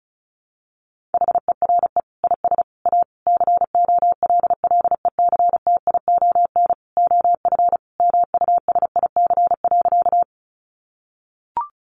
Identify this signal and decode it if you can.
Morse code, heard as a keyed tone, about 34 wpm, 700 Hz. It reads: HERE IS A COLLECTION OF MUSIC.
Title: Sentences from Top 1200 Words 35wpm